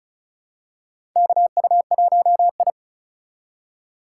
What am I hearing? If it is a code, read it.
Morse code, heard as a keyed tone, about 36 wpm, 700 Hz. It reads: KU1I
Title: USA Call Signs - 4 Characters - 4x Character Spacing - 35wpm